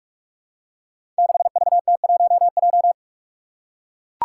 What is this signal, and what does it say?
Morse code, heard as a keyed tone, about 43 wpm, 700 Hz. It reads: 6VT1J
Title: Five Challenging Letter-Number Combinations 45wpm